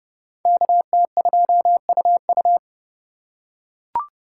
Morse code, heard as a keyed tone, about 29 wpm, 700 Hz. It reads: KT2UU